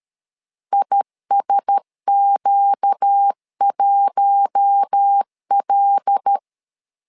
Morse code, heard as a keyed tone, about 13 wpm, 800 Hz. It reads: ISQ1L
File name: LEZIONI / 5 LBVC / 5 L 12wpm 1913